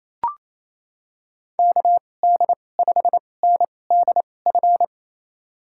Morse code, heard as a keyed tone, about 27 wpm, spaced as for 20 wpm, 700 Hz. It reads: KD5NDF
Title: USA Call Signs - 6 Characters - 2x Character Spacing - 28wpm